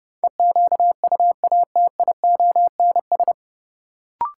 Morse code, heard as a keyed tone, about 31 wpm, 700 Hz. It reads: EQUATIONS
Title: Sets of 5 Words - New Word Drill - 30wpm